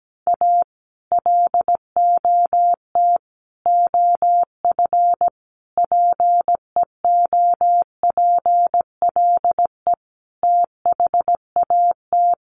Morse code, heard as a keyed tone, about 17 wpm, 700 Hz. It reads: A LOT OF PEOPLE THAT
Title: Sets of 5 Words Encore2 17wpm